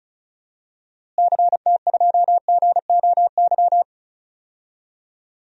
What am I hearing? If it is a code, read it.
Morse code, heard as a keyed tone, about 34 wpm, 700 Hz. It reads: CT2GOY
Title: DXCC-100 Call Signs - 6 Characters - Repeated 2x - 35wpm